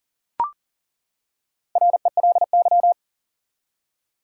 Morse code, heard as a keyed tone, about 40 wpm, 700 Hz. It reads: RELY
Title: Four Letter Words - 40wpm